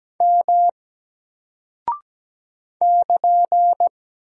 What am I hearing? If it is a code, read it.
Morse code, heard as a keyed tone, about 17 wpm, 700 Hz. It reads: M <KN>
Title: Single Character (Letter:Number:Punctuation:Prosign) 17wpm